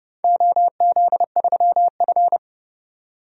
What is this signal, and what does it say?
Morse code, heard as a keyed tone, about 31 wpm, 700 Hz. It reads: OZ3F
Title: DXCC-100 Call Signs - 4 Characters - 30wpm